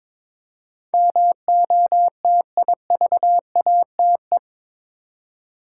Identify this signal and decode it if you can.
Morse code, heard as a keyed tone, about 22 wpm, 700 Hz. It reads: MOTIVATE
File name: practice / 22wpm / Eight Letter Words - 22wpm